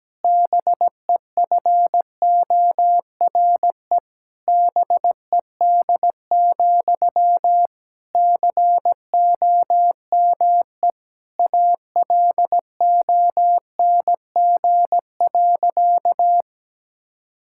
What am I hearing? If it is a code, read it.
Morse code, heard as a keyed tone, about 17 wpm, 700 Hz. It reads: BEFORE BED, COME ALONG.
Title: Sentences from Top 1600 Words 17wpm